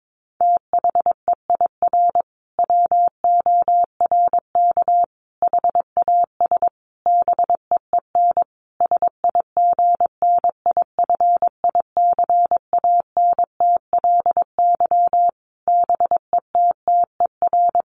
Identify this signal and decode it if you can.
Morse code, heard as a keyed tone, about 22 wpm, 700 Hz. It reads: THEIR WORK HAS BEEN SIGNIFICANTLY BETTER
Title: Sentences from Top 1500 Words 22wpm